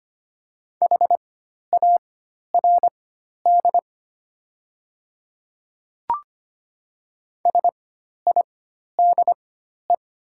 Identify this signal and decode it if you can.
Morse code, heard as a keyed tone, about 25 wpm, spaced as for 12 wpm, 700 Hz. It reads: HARD SIDE
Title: 4x Character Spacing - Top 300 Words - 25wpm